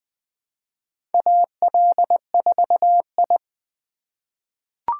Morse code, heard as a keyed tone, about 20 wpm, 700 Hz. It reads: AL4I
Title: USA Call Signs - 4 Characters - 2x Character Spacing - 20wpm